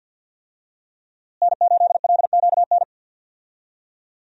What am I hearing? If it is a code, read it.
Morse code, heard as a keyed tone, about 50 wpm, 700 Hz. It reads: N8LQN
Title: USA Call Signs - 5 Characters - Repeated 2x - 50wpm